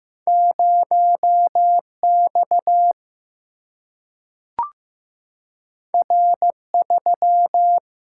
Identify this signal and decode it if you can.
Morse code, heard as a keyed tone, about 15 wpm, 700 Hz. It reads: 0X R3
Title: Single-Letter Single-Number 15wpm